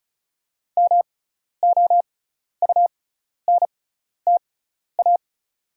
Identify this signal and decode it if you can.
Morse code, heard as a keyed tone, about 34 wpm, spaced as for 12 wpm, 690 Hz. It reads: MOUNTA
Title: 6x Character Spacing - Top 300 Words - 35wpm